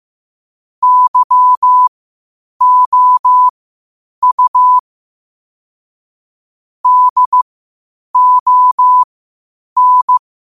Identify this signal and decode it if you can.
Morse code, heard as a keyed tone, about 15 wpm, spaced as for 8 wpm, 1000 Hz. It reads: YOU DON
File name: qotdfiles / 8wpm